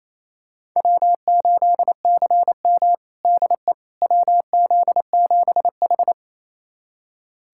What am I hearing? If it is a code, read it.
Morse code, heard as a keyed tone, about 27 wpm, 700 Hz. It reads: W8CM DE WZ7H
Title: USA Callsign DE Callsign - 4 Characters - 28wpm